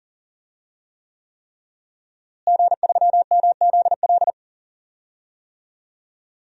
Morse code, heard as a keyed tone, about 40 wpm, 700 Hz. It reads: G3MZL
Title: DXCC-100 Call Signs - 5 Characters - Repeated 3x - 40wpm